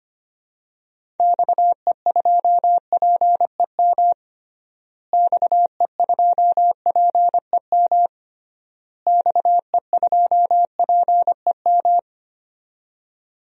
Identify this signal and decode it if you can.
Morse code, heard as a keyed tone, about 25 wpm, 700 Hz. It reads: XE2PEM XE2PEM XE2PEM
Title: DXCC-100 Call Signs - 6 Characters - Repeated 3x - 25wpm